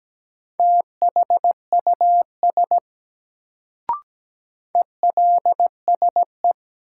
Morse code, heard as a keyed tone, about 17 wpm, 700 Hz. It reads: THUS ELSE
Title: Sentences from Top 800 Words - New Word Drill - 17wpm